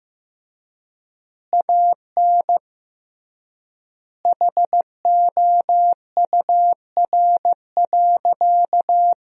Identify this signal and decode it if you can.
Morse code, heard as a keyed tone, about 15 wpm, 700 Hz. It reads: AN HOUR.
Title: 3x Word-Spacing - Sentences from Top 500 Words - 15wpm